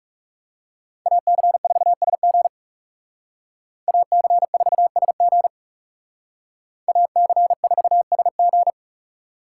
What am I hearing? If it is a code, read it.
Morse code, heard as a keyed tone, about 39 wpm, 700 Hz. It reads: AC4SG AC4SG AC4SG